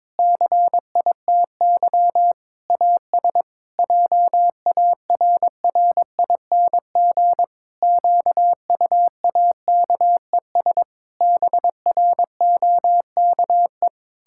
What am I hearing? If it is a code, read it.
Morse code, heard as a keyed tone, about 22 wpm, 700 Hz. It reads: CITY AS JARRING QUAKES BROKE
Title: Pangram Challenge v2 - 22wpm